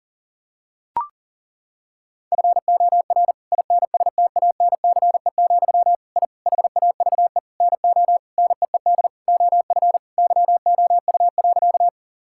Morse code, heard as a keyed tone, about 40 wpm, 700 Hz. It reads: FOR INSTANCE, I HAVE NO NEED OF YOU.
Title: Sentences from Top 1000 Words - Review Part 5 - 40wpm